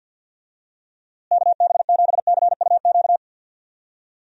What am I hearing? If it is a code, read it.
Morse code, heard as a keyed tone, about 52 wpm, 700 Hz. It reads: KB7CUQ
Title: USA Call Signs - 6 Characters - Repeated 3x - 50wpm